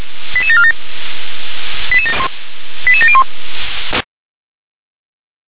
Five-tone selective call (ZVEI), heard as quick runs of five tones, tones of about 80 ms. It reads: 88747 88701
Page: /357/SDR/ZVEI/88154/